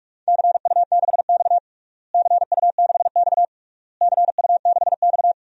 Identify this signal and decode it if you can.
Morse code, heard as a keyed tone, about 46 wpm, 700 Hz. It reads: CU6X CU6X CU6X
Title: DXCC-100 Call Signs - 4 Characters - Repeated 3x - 45wpm